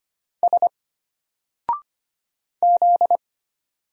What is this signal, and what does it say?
Morse code, heard as a keyed tone, about 25 wpm, 700 Hz. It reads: S Z
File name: Single-Letter-Number-25wpm